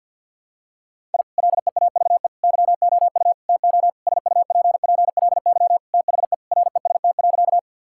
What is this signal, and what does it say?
Morse code, heard as a keyed tone, about 51 wpm, 700 Hz. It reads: I LEAVE YOU TO SUPPLY THE REST.
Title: Sentences from Top 1000 Words - Review Part 5 - 50wpm